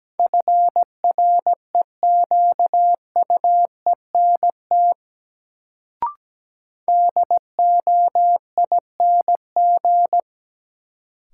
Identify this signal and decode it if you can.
Morse code, heard as a keyed tone, about 17 wpm, 700 Hz. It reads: FREQUENT DOING
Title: Sentences from Top 1400 Words - New Word Drill Alt - 17wpm